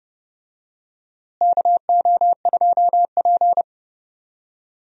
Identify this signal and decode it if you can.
Morse code, heard as a keyed tone, about 30 wpm, 700 Hz. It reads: KO2P
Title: USA Call Signs - 4 Characters - Repeated 2x - 30wpm